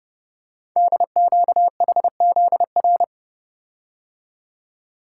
Morse code, heard as a keyed tone, about 28 wpm, 700 Hz. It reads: DQHZR
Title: Code Groups - 5 Letter:Numbers - 30wpm